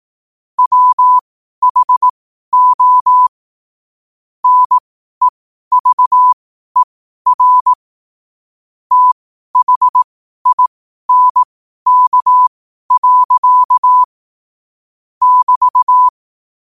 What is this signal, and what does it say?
Morse code, heard as a keyed tone, about 18 wpm, spaced as for 12 wpm, 1000 Hz. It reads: WHO NEVER THINK. <BT>
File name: qotdfiles / 12wpm